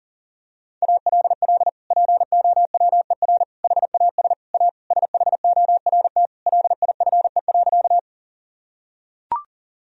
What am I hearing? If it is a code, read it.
Morse code, heard as a keyed tone, about 40 wpm, 700 Hz. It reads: ALL POWER HAS A SHORT LIFE.
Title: Sentences from Top 1000 Words - Review Part 2 - 40wpm